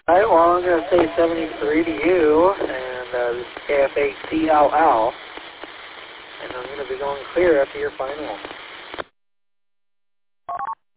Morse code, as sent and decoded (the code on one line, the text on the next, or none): none